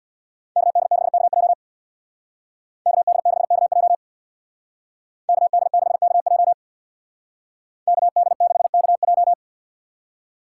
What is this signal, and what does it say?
Morse code, heard as a keyed tone, about 61 wpm, 700 Hz. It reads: KD6KJ KD6KJ KD6KJ KD6KJ
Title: Speed Racing - USA 2x2 Call Signs 50wpm